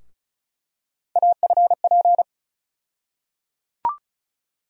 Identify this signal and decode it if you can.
Morse code, heard as a keyed tone, about 36 wpm, 700 Hz. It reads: AFP
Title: USA Call Sign Postfixes - 3 Letters 35wpm